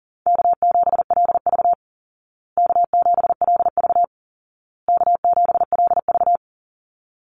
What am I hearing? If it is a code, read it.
Morse code, heard as a keyed tone, about 40 wpm, 700 Hz. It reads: K7LV K7LV K7LV
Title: Call Signs Repeated 2x - 40wpm